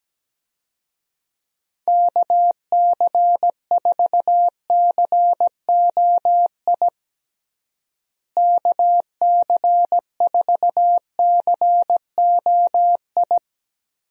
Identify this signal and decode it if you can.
Morse code, heard as a keyed tone, about 17 wpm, 700 Hz. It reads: KC4COI KC4COI
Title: USA Call Signs - 6 Characters - Repeated 2x - 17wpm